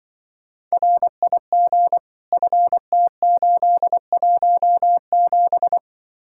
Morse code, heard as a keyed tone, about 24 wpm, 700 Hz. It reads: RIG FT817
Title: Speed-Racing Common QSO Phrases Encore - 20wpm